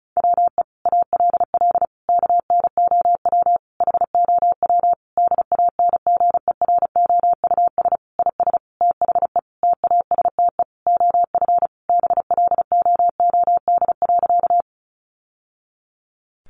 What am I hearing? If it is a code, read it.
Morse code, heard as a keyed tone, about 34 wpm, 700 Hz. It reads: WE ALL KNOW HOW DANGEROUS IS THE TASTE OF BLOOD.
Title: Sentences from Top 1200 Words 35wpm